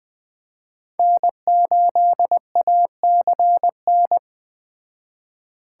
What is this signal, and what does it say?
Morse code, heard as a keyed tone, about 19 wpm, 700 Hz. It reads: N8ACN